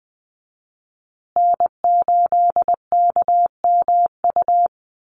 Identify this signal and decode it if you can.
Morse code, heard as a keyed tone, about 20 wpm, 700 Hz. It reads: N8KMU
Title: USA Call Signs Repeated 2x 20wpm